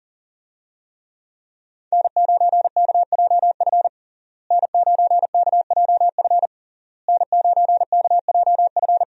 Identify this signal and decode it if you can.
Morse code, heard as a keyed tone, about 40 wpm, 700 Hz. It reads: N9KJF N9KJF N9KJF
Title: USA Call Signs - 5 Characters - Repeated 3x - 40wpm